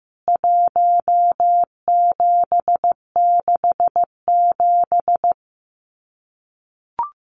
Morse code, heard as a keyed tone, about 15 wpm, 700 Hz. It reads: 1767
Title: CWT Exchange 15wpm